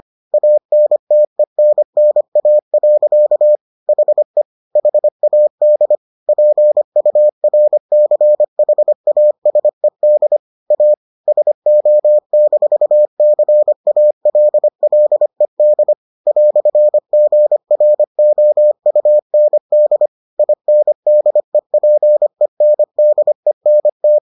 Morse code, heard as a keyed tone, about 25 wpm, 600 Hz. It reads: ANTENNA. HE HAD PURCHASED A SO-CALLED "GROUND INDEPENDENT